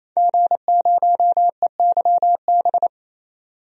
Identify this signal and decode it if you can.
Morse code, heard as a keyed tone, about 28 wpm, 700 Hz. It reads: G0EYB
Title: DXCC-100 Call Signs - 5 Characters - 28wpm